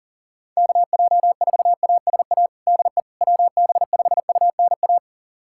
Morse code, heard as a keyed tone, about 40 wpm, 700 Hz. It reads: KJ4ASA DE WB5UNA
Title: USA Callsign DE Callsign - 6 Characters - 40wpm